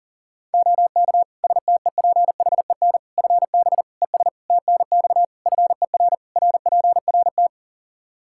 Morse code, heard as a keyed tone, about 40 wpm, 700 Hz. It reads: OK STEPHEN FB ES TNX FER RPRT
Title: Practice QSO - Exchange 2 - Info - 40wpm